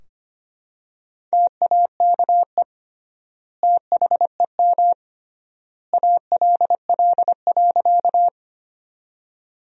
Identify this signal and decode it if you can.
Morse code, heard as a keyed tone, about 25 wpm, 700 Hz. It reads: TAKE THEM ALL.
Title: 3x Word-Spacing - Sentences from Top 200 Words - 25wpm